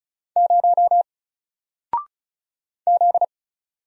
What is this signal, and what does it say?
Morse code, heard as a keyed tone, about 34 wpm, 700 Hz. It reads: 0 Z
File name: Single-Letter-Number-35wpm